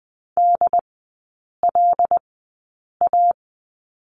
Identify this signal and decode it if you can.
Morse code, heard as a keyed tone, about 20 wpm, spaced as for 8 wpm, 700 Hz. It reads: DLA
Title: 100 Lesson 17e - Letter u - Three Characters 20wpm